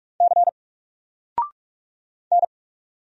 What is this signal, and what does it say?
Morse code, heard as a keyed tone, about 44 wpm, 700 Hz. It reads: C N
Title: Single Character (Letter:Number:Punctuation:Prosign) Alt 45wpm